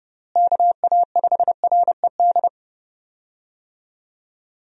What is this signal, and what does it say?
Morse code, heard as a keyed tone, about 30 wpm, 700 Hz. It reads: KA5RED
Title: USA Call Signs - 6 Characters - Repeated 3x - 30wpm